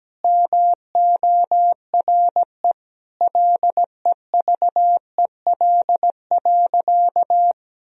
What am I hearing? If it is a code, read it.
Morse code, heard as a keyed tone, about 17 wpm, 700 Hz. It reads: MORE LEVEL.
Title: Sentences from Top 1000 Words - Review Part 4 - 17wpm